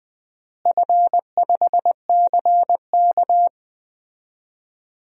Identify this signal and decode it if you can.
Morse code, heard as a keyed tone, about 21 wpm, 700 Hz. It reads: F5CK
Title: DXCC-100 Call Signs - 4 Characters - Repeated 3x - 20wpm